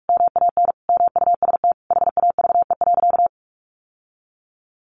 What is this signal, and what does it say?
Morse code, heard as a keyed tone, about 44 wpm, 700 Hz. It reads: MAN MUST HAVE.
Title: Sentences from Top 1100 Words 45wpm